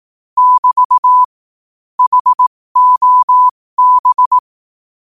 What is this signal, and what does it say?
Morse code, heard as a keyed tone, about 18 wpm, spaced as for 15 wpm, 1000 Hz. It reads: <BT> HOB
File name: qotdfiles / 15wpm